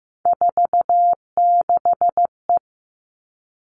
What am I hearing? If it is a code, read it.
Morse code, heard as a keyed tone, about 15 wpm, 700 Hz. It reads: 46E
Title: All Cut Numbers - 3-Digit 15wpm